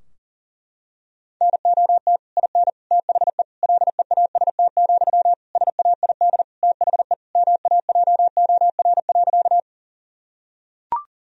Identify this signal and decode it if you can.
Morse code, heard as a keyed tone, about 40 wpm, 700 Hz. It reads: NOT IN THE LEAST, SAID THE MAJOR.
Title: Sentences from Top 1000 Words - Review Part 4 - 40wpm